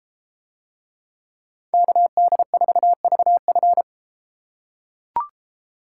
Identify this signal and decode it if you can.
Morse code, heard as a keyed tone, about 31 wpm, 700 Hz. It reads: KD4VF